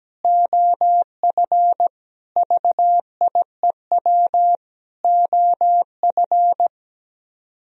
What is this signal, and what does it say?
Morse code, heard as a keyed tone, about 17 wpm, 700 Hz. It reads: OF VIEW OF
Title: Speed-Racing - Sets of 5 Words 17wpm